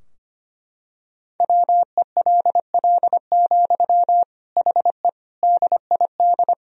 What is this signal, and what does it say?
Morse code, heard as a keyed tone, about 25 wpm, 700 Hz. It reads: WELL, HE DID
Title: Sentences from Top 1000 Words - Review Part 2 - 25wpm